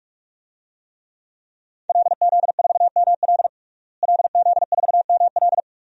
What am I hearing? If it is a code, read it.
Morse code, heard as a keyed tone, about 47 wpm, 700 Hz. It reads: LZ4ML LZ4ML
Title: DXCC-100 Call Signs - 5 Characters - Repeated 2x - 45wpm